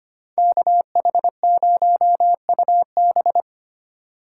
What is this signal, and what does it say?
Morse code, heard as a keyed tone, about 28 wpm, 700 Hz. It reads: KH0UB
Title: Speed Racing - USA 2x2 Call Signs 25wpm